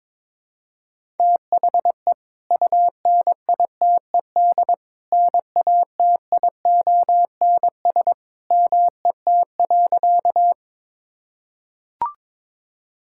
Morse code, heard as a keyed tone, about 22 wpm, 700 Hz. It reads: THE UNITED NATIONS MET.
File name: practice / 22wpm / Sentences with 2-Word Phrases - 22wpm